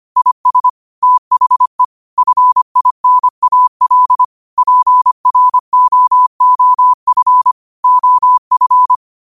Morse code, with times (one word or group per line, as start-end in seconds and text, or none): 0.2-0.7 IS
1.0-1.8 THE
2.2-4.2 FINAL
4.6-7.5 PROOF
7.8-8.9 OF